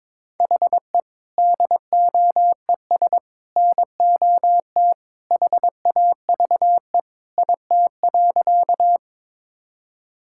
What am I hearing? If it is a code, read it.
Morse code, heard as a keyed tone, about 22 wpm, 700 Hz. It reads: HE DOES NOT HAVE IT.
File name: practice / 22wpm / Sentences with 3-Word Phrases - 22wpm